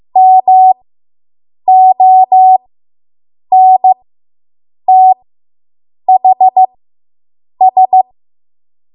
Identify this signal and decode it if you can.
Morse code, heard as a keyed tone, about 15 wpm, spaced as for 7 wpm, 750 Hz. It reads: MONTHS